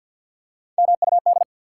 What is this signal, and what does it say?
Morse code, heard as a keyed tone, about 51 wpm, 700 Hz. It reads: MUD
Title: Three Letter Words - 50wpm